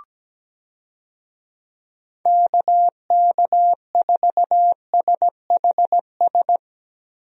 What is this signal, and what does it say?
Morse code, heard as a keyed tone, about 17 wpm, 700 Hz. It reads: KK4SHS
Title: USA Call Signs - 6 Characters - 17wpm